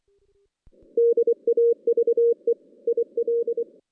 Morse code, heard as a keyed tone, about 24 wpm, 470 Hz. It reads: DAVE IL
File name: 2016 JAN CW NAQP / 2016 Jan naqp k5ot - 0-6